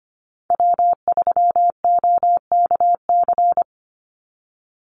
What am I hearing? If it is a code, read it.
Morse code, heard as a keyed tone, about 25 wpm, 700 Hz. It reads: W3OKC
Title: Call Signs - 25wpm